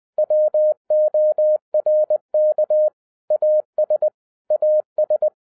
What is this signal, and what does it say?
Morse code, heard as a keyed tone, about 20 wpm, 610 Hz. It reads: WORK AS AS